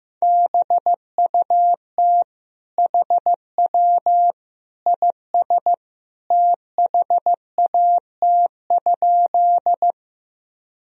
Morse code, heard as a keyed tone, about 15 wpm, 700 Hz. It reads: BUT HW IS THAT?
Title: Sentences from Top 100 Words and Abbreviations 15wpm